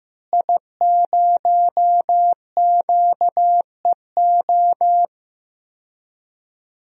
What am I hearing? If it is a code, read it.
Morse code, heard as a keyed tone, about 15 wpm, 700 Hz. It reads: I0QEO
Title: DXCC-100 Call Signs - 5 Characters - Repeated 3x - 15wpm